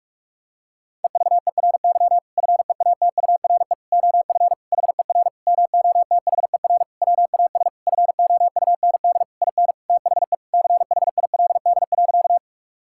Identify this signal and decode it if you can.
Morse code, heard as a keyed tone, about 41 wpm, 700 Hz. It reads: EVERY FEATURE OF HER MOTHER WAS FOUND IN THE CHILD.
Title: Sentences from Top 1000 Words - Review Part 5 - 45wpm